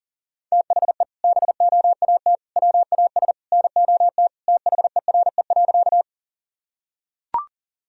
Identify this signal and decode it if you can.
Morse code, heard as a keyed tone, about 37 wpm, 700 Hz. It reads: THE BOAT WAS NOT THERE.
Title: Sentences from Top 1000 Words - Review Part 3 - 40wpm